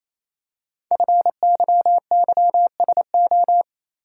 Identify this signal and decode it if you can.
Morse code, heard as a keyed tone, about 28 wpm, 700 Hz. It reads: FYYSO
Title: Code Groups - 5 Letters - 28wpm